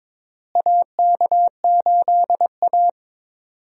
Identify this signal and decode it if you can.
Morse code, heard as a keyed tone, about 21 wpm, 700 Hz. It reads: AK8A
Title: USA Call Signs - 4 Characters - Repeated 3x - 22wpm